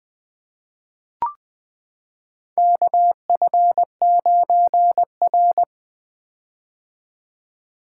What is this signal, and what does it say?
Morse code, heard as a keyed tone, about 21 wpm, 700 Hz. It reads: KF9R
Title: 2023 POTA Hunter Call Signs - 20wpm